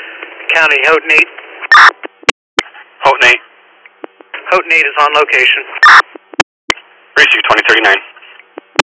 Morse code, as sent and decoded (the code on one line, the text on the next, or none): none